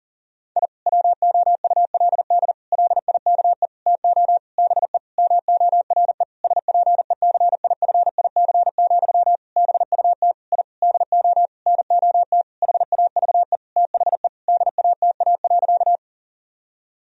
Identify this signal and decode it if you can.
Morse code, heard as a keyed tone, about 40 wpm, 700 Hz. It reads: I WOULD LIKE TO BE MORE SPECIFIC, BUT I DO NOT HAVE THE DATA.
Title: Sentences from Top 1000 Words - Review Part 3 - 40wpm